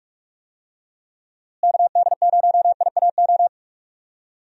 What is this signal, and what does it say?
Morse code, heard as a keyed tone, about 42 wpm, 700 Hz. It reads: KD0IAO
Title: USA Call Signs - 6 Characters - Repeated 3x - 45wpm